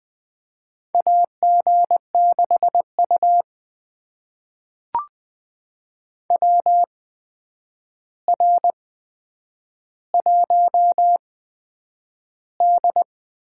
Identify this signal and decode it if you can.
Morse code, heard as a keyed tone, about 20 wpm, spaced as for 9 wpm, 700 Hz. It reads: AG6U WR1D
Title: USA Call Signs - 4 Characters - 8x Character Spacing - 20wpm